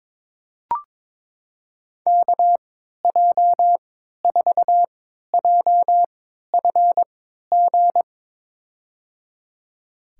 Morse code, heard as a keyed tone, about 22 wpm, spaced as for 13 wpm, 700 Hz. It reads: KJ4JFG
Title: USA Call Signs - 6 Characters - 3x Character Spacing - 22wpm